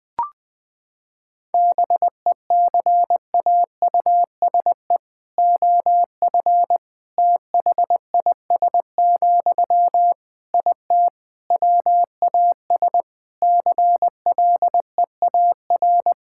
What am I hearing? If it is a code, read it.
Morse code, heard as a keyed tone, about 20 wpm, 700 Hz. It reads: BECAUSE OF THIS, IT WAS CLEAR
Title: Sentences from Top 1000 Words - Review Part 6 - 20wpm